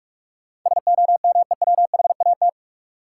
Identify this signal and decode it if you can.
Morse code, heard as a keyed tone, about 46 wpm, 700 Hz. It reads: SOMEWHAT
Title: Sentences from Top 1000 Words - Review Part 1 - New Word Drill - 45wpm